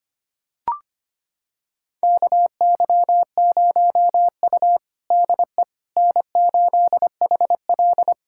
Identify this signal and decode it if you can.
Morse code, heard as a keyed tone, about 25 wpm, 700 Hz. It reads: KY0U DE N8HL
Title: USA Callsign DE Callsign - 4 Characters - 25wpm